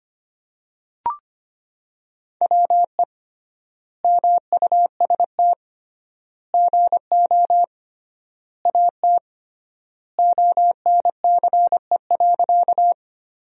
Morse code, heard as a keyed tone, about 25 wpm, 700 Hz. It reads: WE MUST GO AT ONCE.
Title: 3x Word-Spacing - Sentences from Top 300 Words - 25wpm